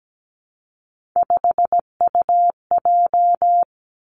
Morse code, heard as a keyed tone, about 17 wpm, 700 Hz. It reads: 5UJ
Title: Three Challenging Letter-Number Combinations 17wpm